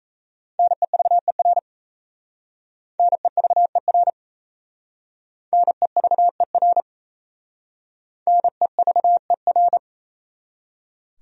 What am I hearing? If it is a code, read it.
Morse code, heard as a keyed tone, about 35 wpm, 700 Hz. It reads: NEVER NEVER NEVER NEVER
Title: Speed-Racing Top 500 Words 28wpm